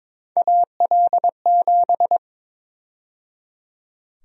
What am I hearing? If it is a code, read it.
Morse code, heard as a keyed tone, about 21 wpm, 700 Hz. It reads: AL7
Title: USA Call Sign Prefixes - 22wpm